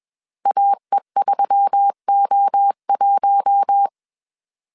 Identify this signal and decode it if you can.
Morse code, heard as a keyed tone, about 21 wpm, 800 Hz. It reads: AE3O1
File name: CW / DET / 1 6 1500C S B 20wpm 2048